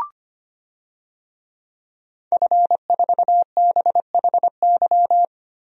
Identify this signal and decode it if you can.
Morse code, heard as a keyed tone, about 25 wpm, 700 Hz. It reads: F4BHY
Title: DXCC-100 Call Signs - 5 Characters - Repeated 3x - 25wpm